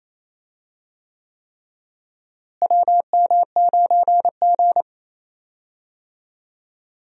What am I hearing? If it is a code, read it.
Morse code, heard as a keyed tone, about 27 wpm, 700 Hz. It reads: WM9G
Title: USA Call Signs - 4 Characters - Repeated 3x - 28wpm